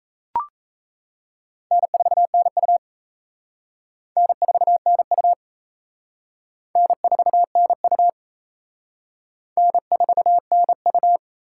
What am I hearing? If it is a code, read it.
Morse code, heard as a keyed tone, about 34 wpm, 700 Hz. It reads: N4NU N4NU N4NU N4NU
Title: Speed Racing - USA 1x2 Call Signs 28wpm